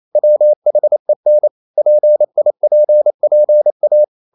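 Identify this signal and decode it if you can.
Morse code, heard as a keyed tone, about 28 wpm, 600 Hz. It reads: WHEN PIPPA
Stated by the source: Morse code practice at 28 wpm, text of Ken Follett's novel Never in English, from CW med Gnister.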